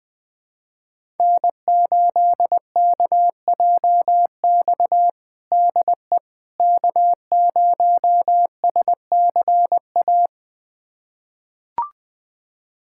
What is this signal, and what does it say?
Morse code, heard as a keyed tone, about 21 wpm, 700 Hz. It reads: N8KJX DE K0SCA
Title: USA Callsign DE Callsign - 5 Characters - 20wpm